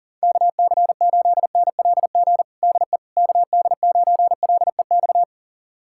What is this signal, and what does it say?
Morse code, heard as a keyed tone, about 40 wpm, 700 Hz. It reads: KC8NLG DE KD9LEX
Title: USA Callsign DE Callsign - 6 Characters - 40wpm